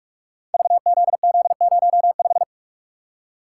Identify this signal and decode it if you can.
Morse code, heard as a keyed tone, about 44 wpm, 700 Hz. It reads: VZZ05
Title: Code Groups - 5 Letter:Numbers - 45wpm